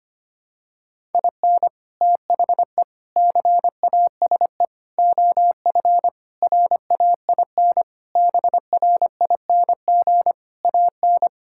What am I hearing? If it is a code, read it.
Morse code, heard as a keyed tone, about 25 wpm, 700 Hz. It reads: IN THE CASE OF RAIN BRING AN
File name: practice / 25wpm / Sentences with 3-Word Phrases - 25wpm